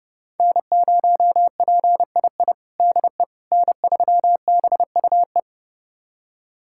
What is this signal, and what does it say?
Morse code, heard as a keyed tone, about 30 wpm, 700 Hz. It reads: N0PII DE N3BUE
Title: USA Callsign DE Callsign - 5 Characters - 30wpm